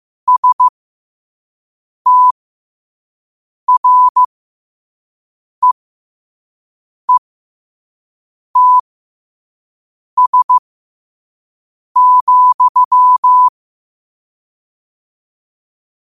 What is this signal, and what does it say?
Morse code, heard as a keyed tone, about 15 wpm, spaced as for 5 wpm, 1000 Hz. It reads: STREETS,
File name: qotdfiles / 5wpm